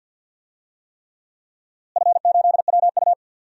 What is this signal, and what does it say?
Morse code, heard as a keyed tone, about 47 wpm, 700 Hz. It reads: F8WU
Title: DXCC-100 Call Signs - 4 Characters - Repeated 3x - 50wpm